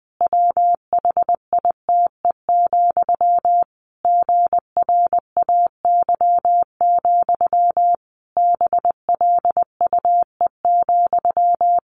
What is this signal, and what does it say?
Morse code, heard as a keyed tone, about 20 wpm, 700 Hz. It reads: WHITE, GRAY, BLUE,
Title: A Sound Of Thunder 20wpm